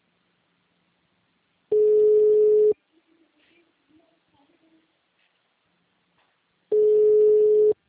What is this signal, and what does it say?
Ringback tone (Europe, 425 Hz).